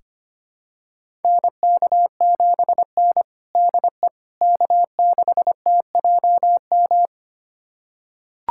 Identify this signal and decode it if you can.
Morse code, heard as a keyed tone, about 25 wpm, 700 Hz. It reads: NK7N DE K6TJM
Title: USA Callsign DE Callsign 25wpm